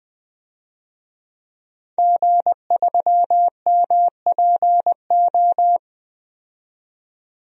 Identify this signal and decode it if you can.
Morse code, heard as a keyed tone, about 20 wpm, 700 Hz. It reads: G3MPO
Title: DXCC-100 Call Signs - 5 Characters - 20wpm